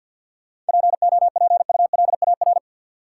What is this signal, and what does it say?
Morse code, heard as a keyed tone, about 53 wpm, 700 Hz. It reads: POPULAR
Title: Sentences from Top 1200 Words - New Word Drill Alt - 50wpm